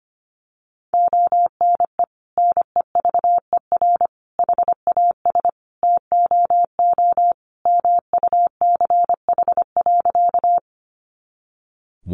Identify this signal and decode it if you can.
Morse code, heard as a keyed tone, about 25 wpm, 700 Hz. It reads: ONE NEVER HAS TOO MUCH.
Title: Sentences from Top 300 Words 25wpm